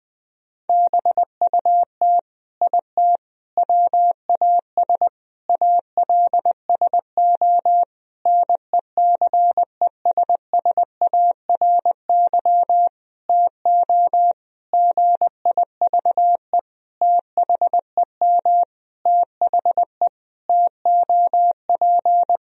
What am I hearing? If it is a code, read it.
Morse code, heard as a keyed tone, about 20 wpm, 700 Hz. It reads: BUT IT WAS ALSO NECESSARY TO GIVE THEM THE TOP
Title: Sentences from Top 1000 Words - Review Part 2 - 20wpm